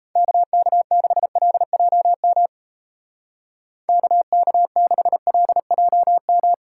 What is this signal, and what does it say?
Morse code, heard as a keyed tone, about 34 wpm, 700 Hz. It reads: KK6LJM KK6LJM
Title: Speed Racing - USA 2x3 Call Signs 28wpm